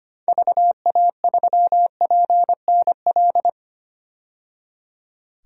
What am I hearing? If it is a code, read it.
Morse code, heard as a keyed tone, about 25 wpm, 700 Hz. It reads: VA3PNL
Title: DXCC-100 Call Signs - 6 Characters - 25wpm